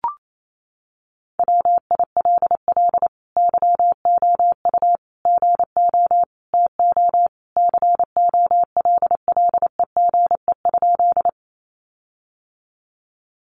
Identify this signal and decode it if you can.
Morse code, heard as a keyed tone, about 28 wpm, 700 Hz. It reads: WILL YOU GO TO COLLEGE?
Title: Sentences from Top 1300 Words 28wpm